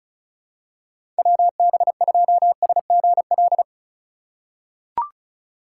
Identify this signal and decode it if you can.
Morse code, heard as a keyed tone, about 36 wpm, 700 Hz. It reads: WB2SGL
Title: USA Call Signs - 6 Characters - 2x Character Spacing - 35wpm